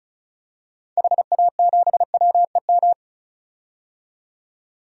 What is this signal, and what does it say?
Morse code, heard as a keyed tone, about 36 wpm, 700 Hz. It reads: HA7WEM
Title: DXCC-100 Call Signs - 6 Characters - 35wpm